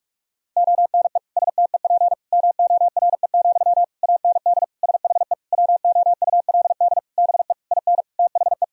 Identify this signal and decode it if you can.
Morse code, heard as a keyed tone, about 41 wpm, 700 Hz. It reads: ONE STEP MORE, AND SHE WOULD BE IN THE
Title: Sentences from Top 1000 Words - Review Part 3 - 45wpm